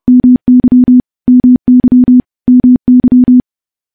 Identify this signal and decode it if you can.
Morse code, heard as a keyed tone, about 28 wpm, 250 Hz. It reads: MY MY MY
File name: morse / morse-training-30WPM